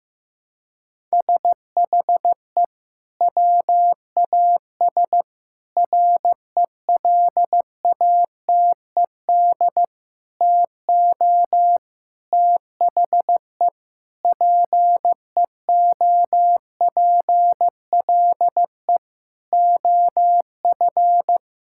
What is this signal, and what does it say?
Morse code, heard as a keyed tone, about 15 wpm, 700 Hz. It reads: SHE WAS RELATED TO THE PEOPLE OF
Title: Sentences from Top 1000 Words - Review Part 5 - 15wpm